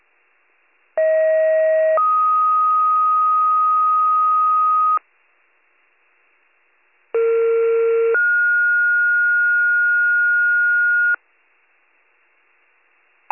Two-tone page, heard as a one-second tone, then a three-second tone, about 640 then 1250 Hz, about 460 then 1430 Hz. It is heard from 1.0 to 11.2 s.